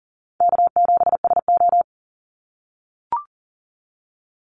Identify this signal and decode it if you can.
Morse code, heard as a keyed tone, about 40 wpm, 700 Hz. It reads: K7SO
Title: NA SOTA Activator Callsigns 40wpm